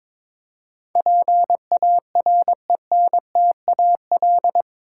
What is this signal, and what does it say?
Morse code, heard as a keyed tone, about 22 wpm, 700 Hz. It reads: PARENTAL
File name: practice / 22wpm / Eight Letter Words - 22wpm